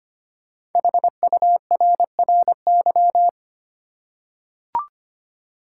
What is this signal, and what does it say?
Morse code, heard as a keyed tone, about 25 wpm, 700 Hz. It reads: HURRY
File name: 5 Letter Words - First 3 Letters Given - 25wpm